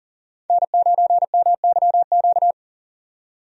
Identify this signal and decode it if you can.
Morse code, heard as a keyed tone, about 40 wpm, 700 Hz. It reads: N9MYQ
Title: USA Call Signs - 5 Characters - 40wpm